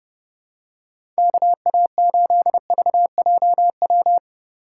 Morse code, heard as a keyed tone, about 30 wpm, 700 Hz. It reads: KA8VJW
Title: USA Call Signs - 6 Characters - 6x Character Spacing - 30wpm